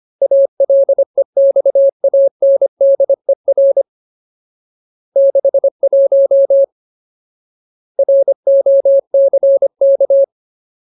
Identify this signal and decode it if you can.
Morse code, heard as a keyed tone, about 25 wpm, 550 Hz. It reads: ALEXANDER 61 ROCK
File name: Ultimate Word List / 25WPM Ultimate